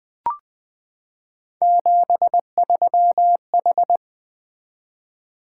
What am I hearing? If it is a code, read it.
Morse code, heard as a keyed tone, about 20 wpm, 700 Hz. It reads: 73H
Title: Three Letter-Number Combinations - 20wpm